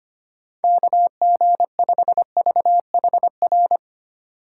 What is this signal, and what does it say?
Morse code, heard as a keyed tone, about 25 wpm, 700 Hz. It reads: KG5VHR
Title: USA Call Signs - 6 Characters - 6x Character Spacing - 25wpm